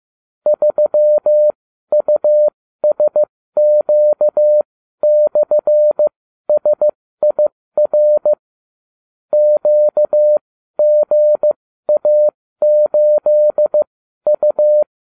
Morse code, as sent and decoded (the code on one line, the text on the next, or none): ...-- ..- ... --.- -..-. ... .. .-. / --.- --. .- ---.. ..-
3USQ/SIR QGA8U